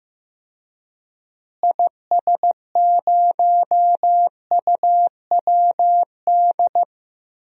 Morse code, heard as a keyed tone, about 15 wpm, 700 Hz. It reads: IS0UWD
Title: DXCC-100 Call Signs - 6 Characters - Repeated 2x - 15wpm